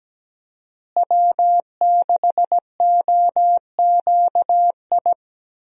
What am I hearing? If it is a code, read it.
Morse code, heard as a keyed tone, about 17 wpm, 700 Hz. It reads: W6OQI
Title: USA Call Signs - 5 Characters - Repeated 3x - 17wpm